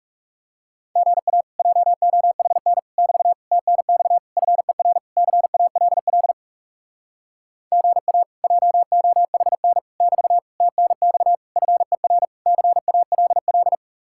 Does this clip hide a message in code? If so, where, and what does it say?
Yes: Morse code from 1.0 to 13.8 s, heard as a keyed tone, about 39 wpm, 700 Hz. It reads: GA JOHN <BT> TNX FER CALL GA JOHN <BT> TNX FER CALL